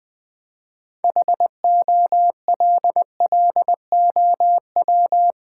Morse code, heard as a keyed tone, about 20 wpm, 700 Hz. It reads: HOLLOW
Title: 6 Letter Words - First 3 Letters Given - 20wpm